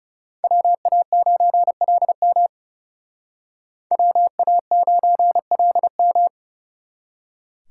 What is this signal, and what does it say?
Morse code, heard as a keyed tone, about 37 wpm, 700 Hz. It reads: WA9LM WA9LM
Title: Speed Racing - USA Call Signs 30wpm